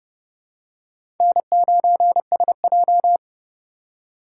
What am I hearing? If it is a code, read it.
Morse code, heard as a keyed tone, about 30 wpm, 700 Hz. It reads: N9SJ
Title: USA Call Signs - 6x Character Spacing - 30wpm